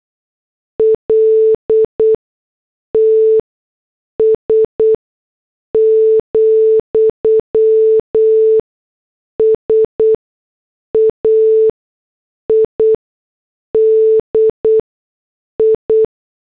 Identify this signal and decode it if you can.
Morse code, heard as a keyed tone, about 8 wpm, spaced as for 6 wpm, 440 Hz. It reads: LTS, SAID I
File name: morse / headlines